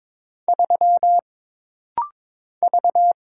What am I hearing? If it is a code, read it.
Morse code, heard as a keyed tone, about 22 wpm, 700 Hz. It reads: 3 V
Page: This rapid-fire timing is geared to push your ICR (Instant Character Recognition) skills.